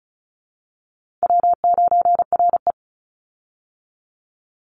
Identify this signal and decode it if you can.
Morse code, heard as a keyed tone, about 34 wpm, 700 Hz. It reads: W9RE
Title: Call Signs Repeated 2x - 35wpm